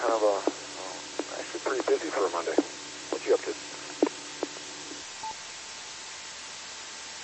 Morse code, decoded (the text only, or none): none